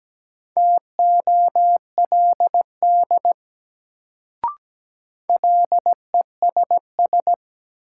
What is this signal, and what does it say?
Morse code, heard as a keyed tone, about 17 wpm, 700 Hz. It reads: TOLD LESS
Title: Sentences from Top 400 Words - New Word Drill - 17wpm